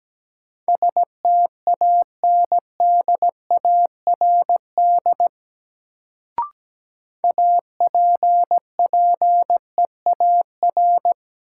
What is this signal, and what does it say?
Morse code, heard as a keyed tone, about 17 wpm, 700 Hz. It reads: STANDARD APPEAR
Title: Sentences from Top 1200 Words - New Word Drill Alt - 17wpm